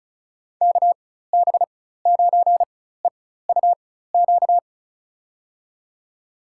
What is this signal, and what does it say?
Morse code, heard as a keyed tone, about 36 wpm, spaced as for 16 wpm, 700 Hz. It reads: KB9EUQ
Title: USA Call Signs - 6 Characters - 4x Character Spacing - 35wpm